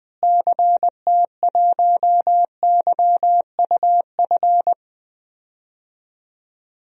Morse code, heard as a keyed tone, about 20 wpm, 700 Hz. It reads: CT1YUF
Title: DXCC-100 Call Signs - 6 Characters - Repeated 3x - 20wpm